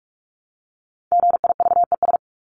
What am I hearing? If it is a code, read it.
Morse code, heard as a keyed tone, about 44 wpm, 700 Hz. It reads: GIVES
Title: ES:S Suffix - Top 100 Words with Base Words 45wpm